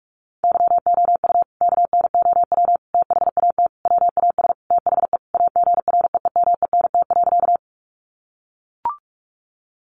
Morse code, heard as a keyed tone, about 44 wpm, 700 Hz. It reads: YOU KNOW THAT WAS THE AGREEMENT.
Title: Sentences from Top 1500 Words 45wpm